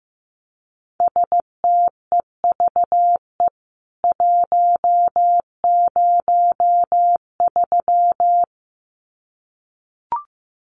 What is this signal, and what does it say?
Morse code, heard as a keyed tone, about 14 wpm, 700 Hz. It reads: STEVE 103
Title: CWT Exchange 15wpm